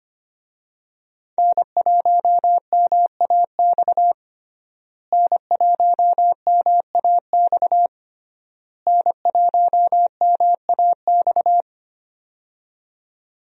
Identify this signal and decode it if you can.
Morse code, heard as a keyed tone, about 25 wpm, 700 Hz. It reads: N1MAX N1MAX N1MAX
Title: USA Call Signs - 5 Characters - Repeated 3x - 25wpm